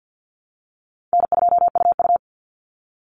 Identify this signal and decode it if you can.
Morse code, heard as a keyed tone, about 50 wpm, 700 Hz. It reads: N2UU
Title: Call Signs Repeated 2x - 50wpm